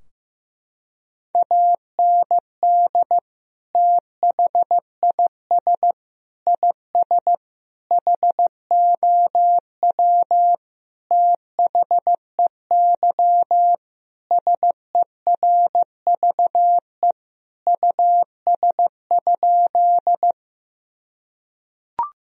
Morse code, heard as a keyed tone, about 15 wpm, 700 Hz. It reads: AND THIS IS HOW THEY SERVE US?
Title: Sentences from Top 1000 Words - Review Part 2 - 15wpm